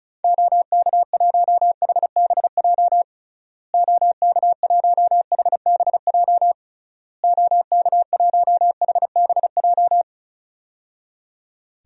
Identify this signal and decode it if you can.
Morse code, heard as a keyed tone, about 36 wpm, 700 Hz. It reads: OK1HBJ OK1HBJ OK1HBJ
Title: DXCC-100 Call Signs - 6 Characters - Repeated 3x - 35wpm